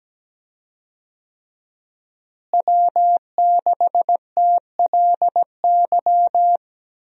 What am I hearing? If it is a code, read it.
Morse code, heard as a keyed tone, about 17 wpm, 700 Hz. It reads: W6TLY
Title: USA Call Signs - 5 Characters - Repeated 3x - 17wpm